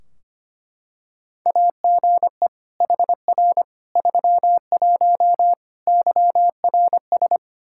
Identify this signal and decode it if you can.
Morse code, heard as a keyed tone, about 25 wpm, 700 Hz. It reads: AGE HR 31 YRS
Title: Practice QSO - Exchange 3 - More Info - 25wpm